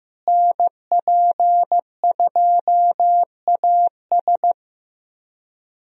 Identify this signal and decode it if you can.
Morse code, heard as a keyed tone, about 15 wpm, 700 Hz. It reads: NP2AS
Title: USA Call Signs - 5 Characters - Repeated 3x - 15wpm